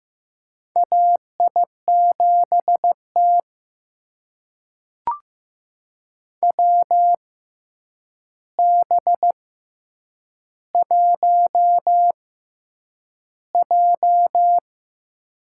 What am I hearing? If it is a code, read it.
Morse code, heard as a keyed tone, about 15 wpm, spaced as for 8 wpm, 700 Hz. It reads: AI7T WB1J
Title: USA Call Signs - 4 Characters - 6x Character Spacing - 15wpm